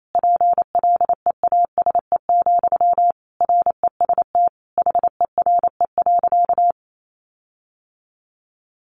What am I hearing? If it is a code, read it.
Morse code, heard as a keyed tone, about 28 wpm, 700 Hz. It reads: PLEASE, REST HERE.
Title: Sentences from Top 1500 Words 28wpm